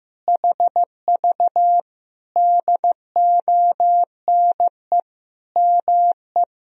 Morse code, heard as a keyed tone, about 15 wpm, 700 Hz. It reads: HV DONE ME
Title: Sentences from Top 500 Words and Abbreviations 15wpm